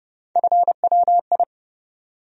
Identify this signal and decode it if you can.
Morse code, heard as a keyed tone, about 31 wpm, 700 Hz. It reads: FWI
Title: USA Call Sign Postfixes - 3 Letters 30wpm